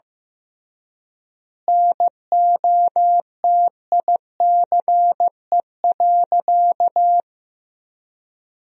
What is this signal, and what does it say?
Morse code, heard as a keyed tone, about 16 wpm, 700 Hz. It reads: NOTICE.
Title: 3x Word-Spacing - Sentences from Top 400 Words - 15wpm